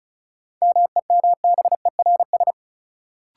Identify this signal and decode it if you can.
Morse code, heard as a keyed tone, about 36 wpm, 700 Hz. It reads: MEMBERS
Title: Sets of 3 Words - New Word Drill - 35wpm